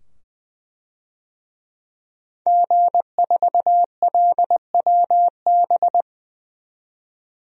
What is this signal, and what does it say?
Morse code, heard as a keyed tone, about 20 wpm, 700 Hz. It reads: G4LWB
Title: DXCC-100 Call Signs - 5 Characters - 20wpm